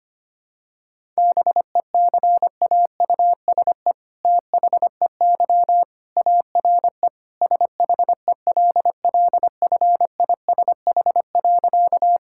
Morse code, heard as a keyed tone, about 25 wpm, 700 Hz. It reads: BECAUSE THEY ARE SHELLFISH.
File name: practice / 25wpm / All-Age Appropriate Jokes - 25wpm